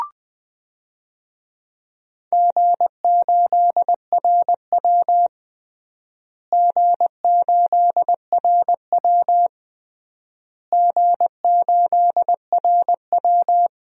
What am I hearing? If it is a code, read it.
Morse code, heard as a keyed tone, about 20 wpm, 700 Hz. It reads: G8RW G8RW G8RW
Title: DXCC-100 Call Signs - 4 Characters - Repeated 3x - 20wpm